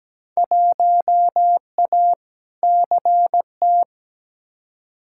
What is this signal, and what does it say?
Morse code, heard as a keyed tone, about 17 wpm, 700 Hz. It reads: 1A CT
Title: Field Day Exchanges - 17wpm